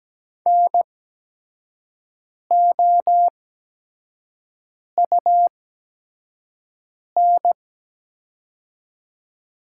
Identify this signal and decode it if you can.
Morse code, heard as a keyed tone, about 17 wpm, spaced as for 5 wpm, 700 Hz. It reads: NOUN
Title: 8x Character Spacing - Top 500 Words - 17wpm